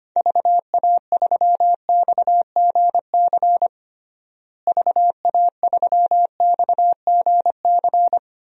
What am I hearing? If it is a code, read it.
Morse code, heard as a keyed tone, about 25 wpm, 700 Hz. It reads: VA3XGC VA3XGC
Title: DXCC-100 Call Signs - 6 Characters - Repeated 3x - 25wpm